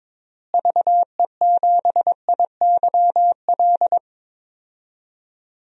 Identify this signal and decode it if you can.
Morse code, heard as a keyed tone, about 22 wpm, 700 Hz. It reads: VE7IYL
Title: DXCC-100 Call Signs - 6 Characters - Repeated 3x - 22wpm